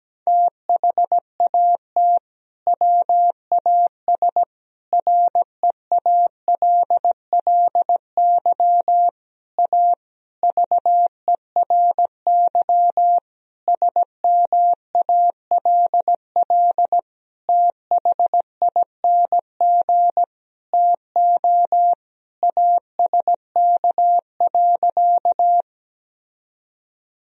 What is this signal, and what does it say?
Morse code, heard as a keyed tone, about 17 wpm, 700 Hz. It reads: THAT WAS REALLY A VERY SMALL THING TO ASK.
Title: Sentences from Top 1000 Words - Review Part 1 - 17wpm